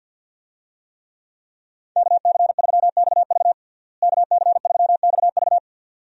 Morse code, heard as a keyed tone, about 52 wpm, 700 Hz. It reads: KC3XV KC3XV
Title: USA Call Signs - 5 Characters - Repeated 2x - 50wpm